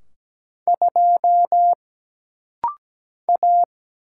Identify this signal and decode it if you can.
Morse code, heard as a keyed tone, about 17 wpm, 700 Hz. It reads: 2 A
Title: Single Letter-Number - Rapid-Fire 17wpm